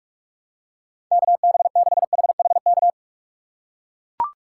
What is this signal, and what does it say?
Morse code, heard as a keyed tone, about 44 wpm, 700 Hz. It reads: KB6HHK
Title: USA Call Signs - 6 Characters - 6x Character Spacing - 45wpm